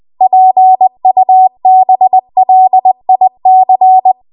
Morse code, heard as a keyed tone, about 20 wpm, 750 Hz. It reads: PUBLIC
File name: Morse / 20WPM